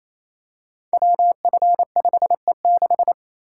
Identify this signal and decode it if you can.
Morse code, heard as a keyed tone, about 29 wpm, 700 Hz. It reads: WF5E6
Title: Code Groups - 5 Letter:Numbers - 28wpm